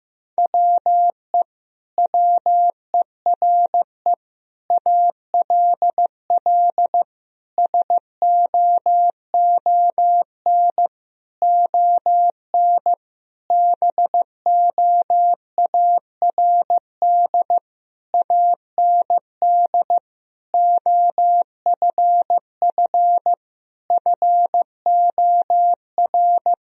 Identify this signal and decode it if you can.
Morse code, heard as a keyed tone, about 15 wpm, 700 Hz. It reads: WE WERE ALL SOON ON BOARD AND OFF FOR
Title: Sentences from Top 1000 Words - Review Part 4 - 15wpm